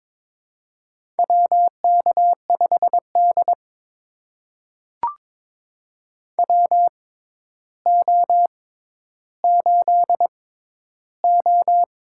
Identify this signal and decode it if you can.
Morse code, heard as a keyed tone, about 22 wpm, spaced as for 11 wpm, 700 Hz. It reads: WK5D WO8O
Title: USA Call Signs - 4 Characters - 6x Character Spacing - 22wpm